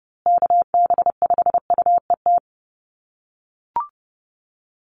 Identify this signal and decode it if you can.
Morse code, heard as a keyed tone, about 30 wpm, 700 Hz. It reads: KB5UET
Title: USA Call Signs Repeated 3x 30wpm